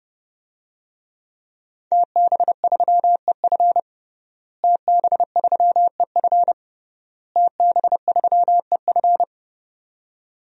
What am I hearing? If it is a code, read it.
Morse code, heard as a keyed tone, about 32 wpm, 700 Hz. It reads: TB3EF TB3EF TB3EF
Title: DXCC-100 Call Signs - 5 Characters - Repeated 3x - 30wpm